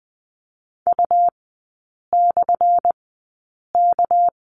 Morse code, heard as a keyed tone, about 19 wpm, spaced as for 8 wpm, 700 Hz. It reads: U/K
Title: 218 Lesson 35e - Number 0 - Three Characters 20wpm